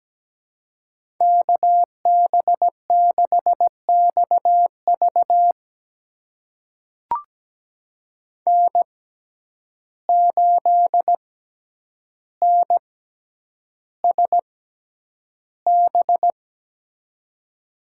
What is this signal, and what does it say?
Morse code, heard as a keyed tone, about 17 wpm, spaced as for 9 wpm, 700 Hz. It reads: KB6XV N8NSB
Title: USA Call Signs - 5 Characters - 6x Character Spacing - 17wpm